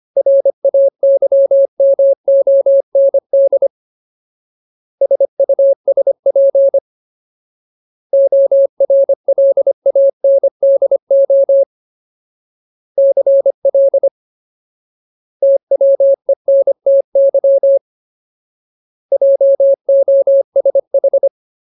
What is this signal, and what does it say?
Morse code, heard as a keyed tone, about 25 wpm, 550 Hz. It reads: RAYMOND SUSP ORLANDO CL TWENTY JOSH
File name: Ultimate Word List / 25WPM Ultimate